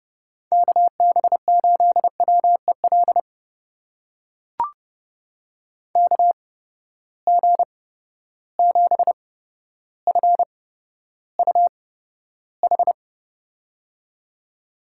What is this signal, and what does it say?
Morse code, heard as a keyed tone, about 31 wpm, spaced as for 13 wpm, 700 Hz. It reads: KB8WEL KG7FUH
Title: USA Call Signs - 6 Characters - 8x Character Spacing - 30wpm